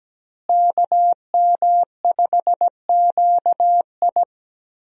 Morse code, heard as a keyed tone, about 17 wpm, 700 Hz. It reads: KM5QI
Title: USA Call Signs - 5 Characters - Repeated 3x - 17wpm